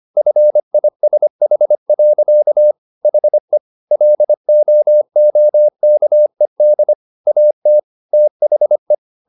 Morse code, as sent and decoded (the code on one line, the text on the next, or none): ..-. .. ... .... .-.-.- / .... / .-.. --- --- -.- . -.. / .- - / - .... .
FISH. HE LOOKED AT THE